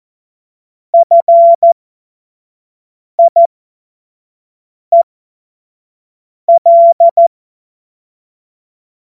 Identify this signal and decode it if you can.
Morse code, heard as a keyed tone, about 14 wpm, spaced as for 5 wpm, 680 Hz. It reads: FIEL
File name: Morsekurs